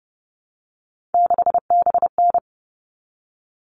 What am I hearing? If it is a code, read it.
Morse code, heard as a keyed tone, about 30 wpm, 700 Hz. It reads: 6BN
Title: All Cut Numbers - 3-Digit 30wpm